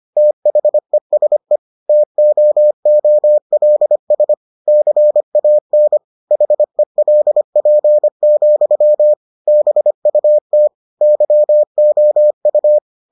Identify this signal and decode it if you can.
Morse code, heard as a keyed tone, about 25 wpm, 600 Hz. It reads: THESE TOOLS CAN HELP, BUT YOU